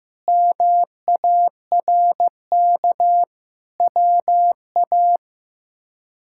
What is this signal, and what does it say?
Morse code, heard as a keyed tone, about 15 wpm, 700 Hz. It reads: MARK WA
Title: K1USN SST QSO Practice - 15wpm